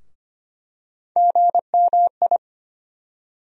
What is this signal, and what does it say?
Morse code, heard as a keyed tone, about 25 wpm, 700 Hz. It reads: GMI
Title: USA Call Sign Postfixes - 3 Letters 25wpm